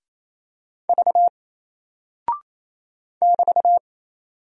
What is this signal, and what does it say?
Morse code, heard as a keyed tone, about 27 wpm, 700 Hz. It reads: V <BT>
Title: Single Character (Letter:Number:Punctuation:Prosign) 28wpm